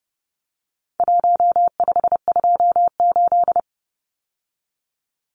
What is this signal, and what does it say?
Morse code, heard as a keyed tone, about 30 wpm, 700 Hz. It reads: 1528
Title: Numbers - Contest Style - 4-Digit 30wpm